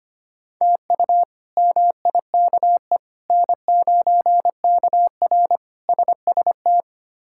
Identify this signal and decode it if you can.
Morse code, heard as a keyed tone, about 25 wpm, 700 Hz. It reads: TU MIKE N9KR SST
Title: K1USN SST QSO Practice - 25wpm